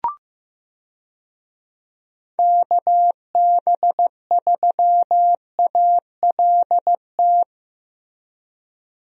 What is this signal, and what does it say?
Morse code, heard as a keyed tone, about 15 wpm, 700 Hz. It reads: KB3ALT
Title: USA Call Signs - 6 Characters - Repeated 3x - 15wpm